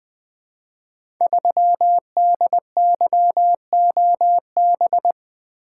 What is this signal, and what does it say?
Morse code, heard as a keyed tone, about 20 wpm, 700 Hz. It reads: 3DYOB
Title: Code Groups - 5 Letter:Numbers - 20wpm